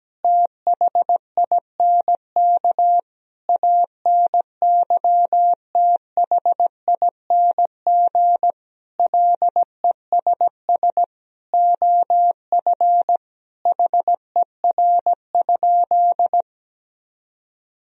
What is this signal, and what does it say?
Morse code, heard as a keyed tone, about 17 wpm, 700 Hz. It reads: THINK ANYTHING LESS OF HER?
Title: Sentences from Top 1000 Words - Review Part 2 - 17wpm